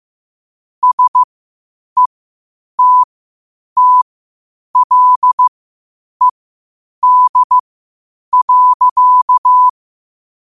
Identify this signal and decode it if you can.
Morse code, heard as a keyed tone, about 15 wpm, spaced as for 8 wpm, 1000 Hz. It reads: SETTLED.
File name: qotdfiles / 8wpm